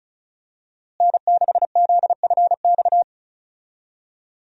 Morse code, heard as a keyed tone, about 36 wpm, 700 Hz. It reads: N6ZFX
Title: USA Call Signs - 5 Characters - Repeated 3x - 35wpm